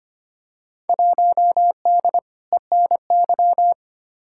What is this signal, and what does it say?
Morse code, heard as a keyed tone, about 25 wpm, 700 Hz. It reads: 1D ENY
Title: Field Day Exchanges - 25wpm